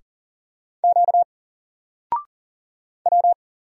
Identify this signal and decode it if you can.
Morse code, heard as a keyed tone, about 40 wpm, 700 Hz. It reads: Q W
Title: Single Letter-Number 40wpm